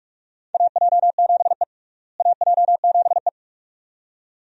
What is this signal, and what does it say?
Morse code, heard as a keyed tone, about 46 wpm, 700 Hz. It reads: AJ7E AJ7E
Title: USA Call Signs - 4 Characters - Repeated 2x - 45wpm